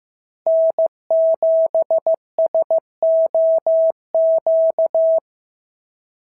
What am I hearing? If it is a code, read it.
Morse code, heard as a keyed tone, about 14 wpm, 650 Hz. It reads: N7SOQ
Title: SOTA QSO Practice 15wpm